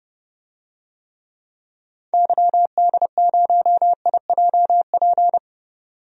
Morse code, heard as a keyed tone, about 30 wpm, 700 Hz. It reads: YD0IJP
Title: DXCC-100 Call Signs - 6 Characters - 30wpm